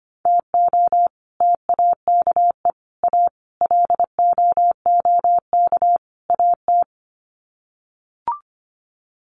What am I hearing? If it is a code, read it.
Morse code, heard as a keyed tone, about 25 wpm, 700 Hz. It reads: TO TAKE A LOOK AT
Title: Sets of 5 Words 25wpm